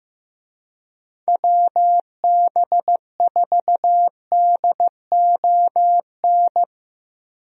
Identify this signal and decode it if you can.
Morse code, heard as a keyed tone, about 15 wpm, 700 Hz. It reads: WB4DON
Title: USA Call Signs - 6 Characters - Repeated 2x - 15wpm